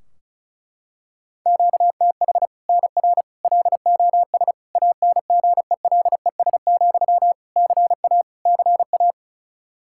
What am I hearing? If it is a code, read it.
Morse code, heard as a keyed tone, about 36 wpm, 700 Hz. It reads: QTH NR LOS ANGELES, CA CA
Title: Practice QSO - Exchange 1 - Beginning - 35wpm